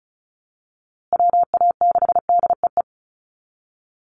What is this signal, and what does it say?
Morse code, heard as a keyed tone, about 34 wpm, 700 Hz. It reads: WA6DEE
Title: USA Call Signs Repeated 2x 35wpm